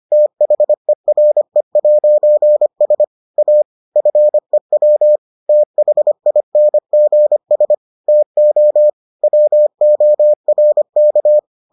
Morse code, heard as a keyed tone, about 25 wpm, 600 Hz. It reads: THERE'S A FEW THINGS TO WORK